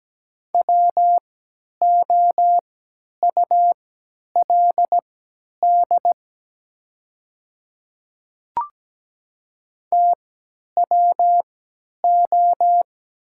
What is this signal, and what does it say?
Morse code, heard as a keyed tone, about 17 wpm, spaced as for 10 wpm, 700 Hz. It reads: WOULD TWO
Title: 3x Character Spacing - Top 100 Words - 17wpm